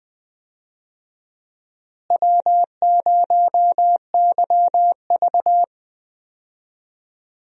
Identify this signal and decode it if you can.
Morse code, heard as a keyed tone, about 20 wpm, 700 Hz. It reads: W0YV
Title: USA Call Signs - 4 Characters - Repeated 2x - 20wpm